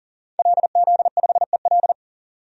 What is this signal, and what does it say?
Morse code, heard as a keyed tone, about 37 wpm, 700 Hz. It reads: LZ5EL
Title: DXCC-100 Call Signs - 5 Characters - 40wpm